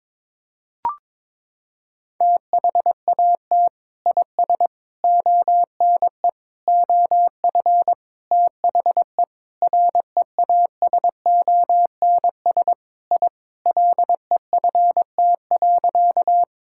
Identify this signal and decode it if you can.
Morse code, heard as a keyed tone, about 22 wpm, 700 Hz. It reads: THAT IS ONE OF THE REASONS I LEFT.
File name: practice / 22wpm / Sentences with 4-Word Phrases - 22wpm